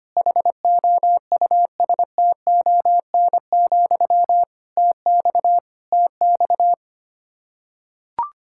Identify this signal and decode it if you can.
Morse code, heard as a keyed tone, about 25 wpm, 700 Hz. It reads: HOUSTON, TX TX
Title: Practice QSO - Exchange 1 - Beginning - 25wpm